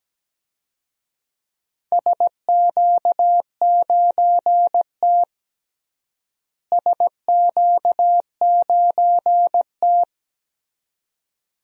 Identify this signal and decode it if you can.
Morse code, heard as a keyed tone, about 17 wpm, 700 Hz. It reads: SQ9T SQ9T
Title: DXCC-100 Call Signs - 4 Characters - Repeated 2x - 17wpm